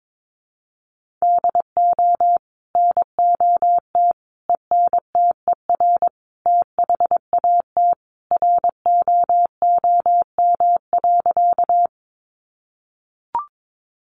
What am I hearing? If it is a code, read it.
Morse code, heard as a keyed tone, about 22 wpm, 700 Hz. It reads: DO NOT ENTER THAT ROOM.
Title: Sentences from Top 1300 Words 22wpm